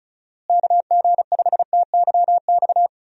Morse code, heard as a keyed tone, about 36 wpm, 700 Hz. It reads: KG5TYX
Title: USA Call Signs - 6 Characters - Repeated 3x - 35wpm